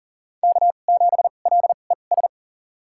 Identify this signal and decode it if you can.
Morse code, heard as a keyed tone, about 37 wpm, spaced as for 28 wpm, 700 Hz. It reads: K7LES
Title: USA Call Signs - 5 Characters - 2x Character Spacing - 40wpm